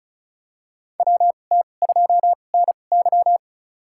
Morse code, heard as a keyed tone, about 36 wpm, spaced as for 25 wpm, 700 Hz. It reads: WT2NY
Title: USA Call Signs - 5 Characters - 2x Character Spacing - 35wpm